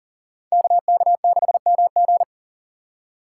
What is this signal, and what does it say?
Morse code, heard as a keyed tone, about 40 wpm, 700 Hz. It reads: KK6MG
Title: USA Call Signs - 5 Characters - 6x Character Spacing - 40wpm